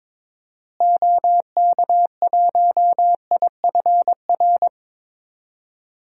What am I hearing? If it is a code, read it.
Morse code, heard as a keyed tone, about 22 wpm, 700 Hz. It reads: OK1IFR
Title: DXCC-100 Call Signs - 6 Characters - Repeated 3x - 22wpm